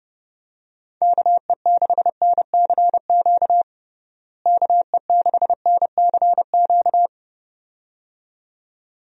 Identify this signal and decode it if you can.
Morse code, heard as a keyed tone, about 30 wpm, 700 Hz. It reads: KE6NCQ KE6NCQ
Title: USA Call Signs - 6 Characters - Repeated 2x - 30wpm